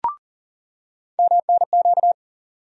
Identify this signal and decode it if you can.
Morse code, heard as a keyed tone, about 40 wpm, 700 Hz. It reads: MNQ
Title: USA Call Sign Postfixes - 3 Letters 40wpm